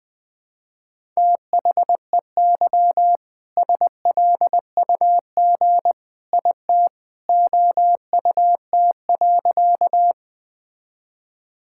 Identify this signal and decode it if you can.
Morse code, heard as a keyed tone, about 20 wpm, 700 Hz. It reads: THEY SLUG IT OUT.
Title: All-Age Appropriate Jokes - 20wpm